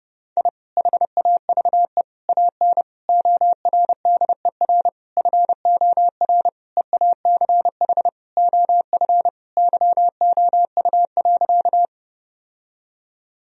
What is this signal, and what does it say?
Morse code, heard as a keyed tone, about 30 wpm, 700 Hz. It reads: I HAVE AN ORDER FOR EACH OF YOU.
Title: Sentences from Top 1000 Words - Review Part 2 - 30wpm